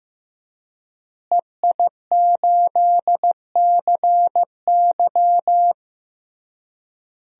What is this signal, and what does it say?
Morse code, heard as a keyed tone, about 15 wpm, 700 Hz. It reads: EI8CY
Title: DXCC-100 Call Signs - 5 Characters - Repeated 3x - 15wpm